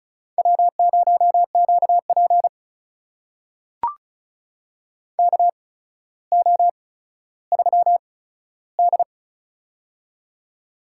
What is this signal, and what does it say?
Morse code, heard as a keyed tone, about 36 wpm, spaced as for 15 wpm, 700 Hz. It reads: W0QP KO3D
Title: USA Call Signs - 4 Characters - 8x Character Spacing - 35wpm